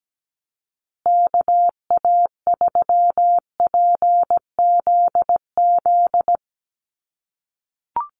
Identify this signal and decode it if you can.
Morse code, heard as a keyed tone, about 17 wpm, 700 Hz. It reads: KA3PZZ
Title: USA Call Signs Repeated 2x 17wpm